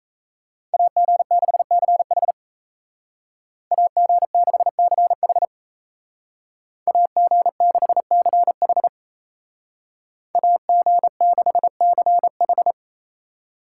Morse code, heard as a keyed tone, about 34 wpm, 700 Hz. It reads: AG6CH AG6CH AG6CH AG6CH
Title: Speed Racing - USA Call Signs 28wpm